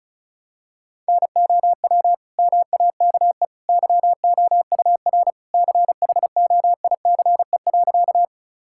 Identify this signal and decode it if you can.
Morse code, heard as a keyed tone, about 36 wpm, 700 Hz. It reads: NOW MAKE YOUR CHOICE.
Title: Sentences from Top 1000 Words - Review Part 6 - 35wpm